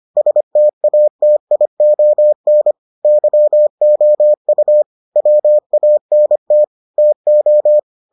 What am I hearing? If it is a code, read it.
Morse code, heard as a keyed tone, about 25 wpm, 600 Hz. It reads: STATION YOU WANT TO